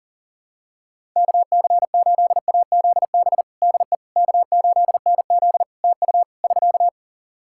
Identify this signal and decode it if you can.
Morse code, heard as a keyed tone, about 37 wpm, 700 Hz. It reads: KC8AZB DE K8NZ TU <SK>
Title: Practice QSO - Exchange 4 - Ending - 40wpm